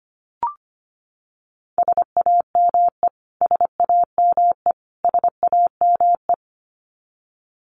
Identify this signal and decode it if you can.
Morse code, heard as a keyed tone, about 25 wpm, 700 Hz. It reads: SAME SAME SAME
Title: Top 500 Words Repeated 25wpm